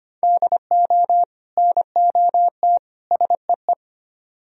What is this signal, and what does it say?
Morse code, heard as a keyed tone, about 33 wpm, 700 Hz. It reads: DO NOT SEE